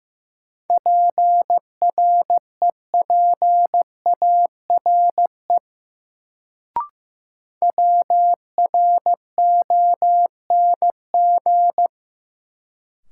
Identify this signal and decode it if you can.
Morse code, heard as a keyed tone, about 15 wpm, 700 Hz. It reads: PREPARE WRONG
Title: Sentences from Top 900 Words - New Word Drill - 15wpm